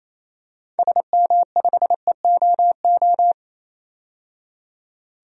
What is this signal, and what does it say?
Morse code, heard as a keyed tone, about 28 wpm, 700 Hz. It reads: SM5EOO
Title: DXCC-100 Call Signs - 6 Characters - 28wpm